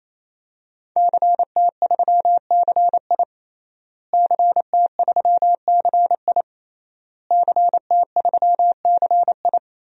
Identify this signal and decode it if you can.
Morse code, heard as a keyed tone, about 28 wpm, 700 Hz. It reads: CT3CI CT3CI CT3CI
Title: DXCC-100 Call Signs - 5 Characters - Repeated 3x - 28wpm